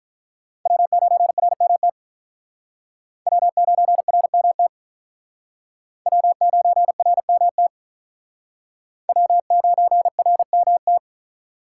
Morse code, heard as a keyed tone, about 47 wpm, 700 Hz. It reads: W9RMT W9RMT W9RMT W9RMT